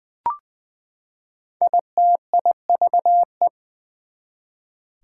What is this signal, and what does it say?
Morse code, heard as a keyed tone, about 20 wpm, 700 Hz. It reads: ITIVE
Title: Prefixes and Suffixes 20wpm